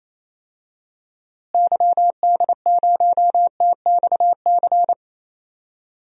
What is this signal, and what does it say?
Morse code, heard as a keyed tone, about 28 wpm, 700 Hz. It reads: YD0TXC
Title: DXCC-100 Call Signs - 6 Characters - Repeated 3x - 28wpm